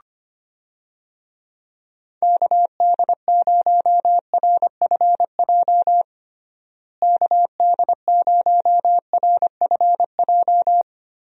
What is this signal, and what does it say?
Morse code, heard as a keyed tone, about 25 wpm, 700 Hz. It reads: KD0RFJ KD0RFJ
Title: USA Call Signs - 6 Characters - Repeated 3x - 25wpm